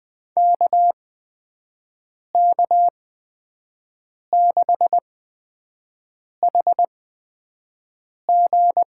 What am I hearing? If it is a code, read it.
Morse code, heard as a keyed tone, about 20 wpm, spaced as for 5 wpm, 700 Hz. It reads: KK6HG
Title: USA Call Signs - 6 Characters - 8x Character Spacing - 20wpm